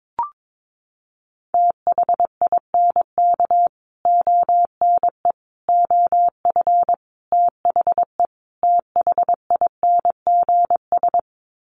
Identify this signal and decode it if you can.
Morse code, heard as a keyed tone, about 22 wpm, 700 Hz. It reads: THINK ONE OF THE THINGS
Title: Sets of 5 Words Encore2 22wpm